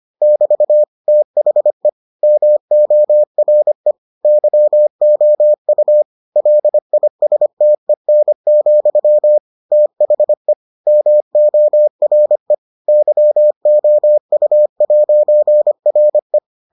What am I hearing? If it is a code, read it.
Morse code, heard as a keyed tone, about 25 wpm, 600 Hz. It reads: <BT> THE MORE YOU LISTEN, THE MORE YOU'RE